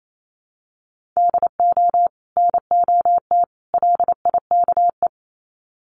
Morse code, heard as a keyed tone, about 28 wpm, 700 Hz. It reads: DO NOT LIKE
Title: Sets of 3 Words 28wpm